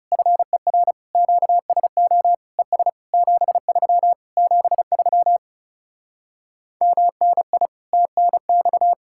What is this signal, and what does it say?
Morse code, heard as a keyed tone, about 40 wpm, 700 Hz. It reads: FER QSO ES 73 73 MNI TNX